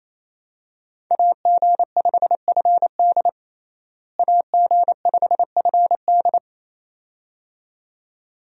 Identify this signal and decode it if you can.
Morse code, heard as a keyed tone, about 28 wpm, 700 Hz. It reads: AG5FD AG5FD
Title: USA Call Signs - 5 Characters - Repeated 2x - 28wpm